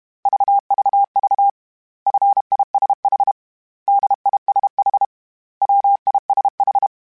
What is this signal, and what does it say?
Morse code, heard as a keyed tone, about 32 wpm, 800 Hz. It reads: VVV FISH DISH WISH